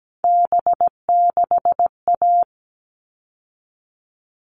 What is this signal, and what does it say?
Morse code, heard as a keyed tone, about 17 wpm, 700 Hz. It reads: B6A
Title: All Cut Numbers - 3-Digit 17wpm